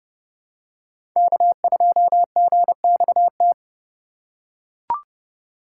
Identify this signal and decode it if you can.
Morse code, heard as a keyed tone, about 29 wpm, 700 Hz. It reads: K2GXT
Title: USA Call Signs - 2x Character Spacing - 30wpm